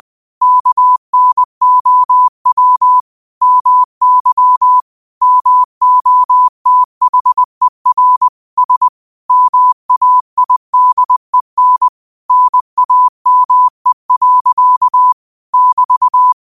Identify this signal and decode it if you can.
Morse code, heard as a keyed tone, about 20 wpm, 1000 Hz. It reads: KNOW MY MOTHER S MAIDEN NAME. <BT>
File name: qotdfiles / 20wpm